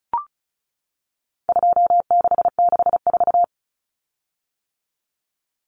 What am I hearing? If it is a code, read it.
Morse code, heard as a keyed tone, about 34 wpm, 700 Hz. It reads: 2664
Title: Common Cut Numbers - 4-Digit 35wpm